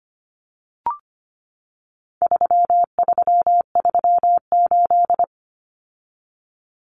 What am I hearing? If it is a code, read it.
Morse code, heard as a keyed tone, about 25 wpm, 700 Hz. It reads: 3338
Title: Common Cut Numbers - 4-Digit 25wpm